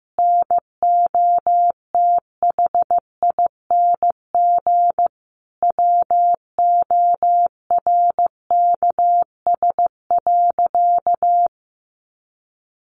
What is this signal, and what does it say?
Morse code, heard as a keyed tone, about 15 wpm, 700 Hz. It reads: NOTHING WORKS.
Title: Additional Easy Phrases 15wpm